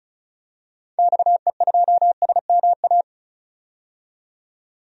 Morse code, heard as a keyed tone, about 36 wpm, 690 Hz. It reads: XE2SMA
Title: DXCC-100 Call Signs - 6 Characters - 35wpm